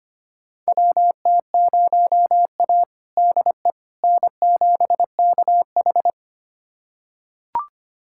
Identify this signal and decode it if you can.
Morse code, heard as a keyed tone, about 25 wpm, 700 Hz. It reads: WT0A DE N7KH
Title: USA Callsign DE Callsign - 4 Characters - 25wpm